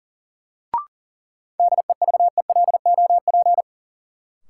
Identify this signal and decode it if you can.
Morse code, heard as a keyed tone, about 40 wpm, 700 Hz. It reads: DEVELOP
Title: Sentences from Top 1000 Words - Review Part 2 - New Word Drill - 40wpm